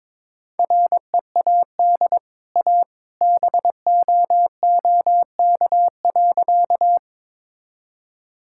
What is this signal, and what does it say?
Morse code, heard as a keyed tone, about 22 wpm, 690 Hz. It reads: READ A BOOK.
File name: practice / 22wpm / Easy Sentences with Hints - 22wpm